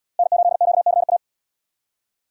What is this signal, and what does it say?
Morse code, heard as a keyed tone, about 70 wpm, 700 Hz. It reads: N8QCA